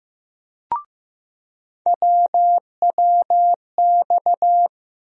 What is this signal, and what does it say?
Morse code, heard as a keyed tone, about 15 wpm, 700 Hz. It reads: WWX
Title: USA Call Sign Postfixes - 3 Letters 15wpm